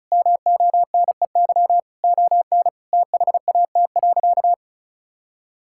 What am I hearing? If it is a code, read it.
Morse code, heard as a keyed tone, about 36 wpm, 700 Hz. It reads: MONEY ON THAT.
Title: Sentences from Top 1000 Words - Review Part 5 - 35wpm